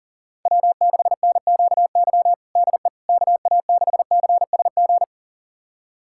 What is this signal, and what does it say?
Morse code, heard as a keyed tone, about 37 wpm, 700 Hz. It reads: W6NQY DE KA6CSG
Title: USA Callsign DE Callsign 40wpm